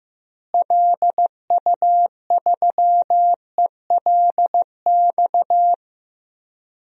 Morse code, heard as a keyed tone, about 15 wpm, 700 Hz. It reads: LU3ELX
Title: DXCC-100 Call Signs - 6 Characters - Repeated 3x - 15wpm